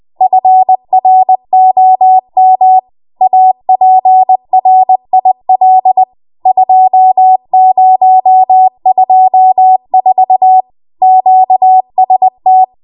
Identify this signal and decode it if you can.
Morse code, heard as a keyed tone, about 20 wpm, 750 Hz. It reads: FROM APRIL 2024 QST
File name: Morse / 20WPM